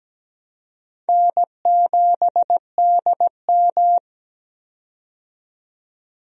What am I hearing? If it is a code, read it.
Morse code, heard as a keyed tone, about 17 wpm, 700 Hz. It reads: N7DM
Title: USA Call Signs - 4 Characters - 17wpm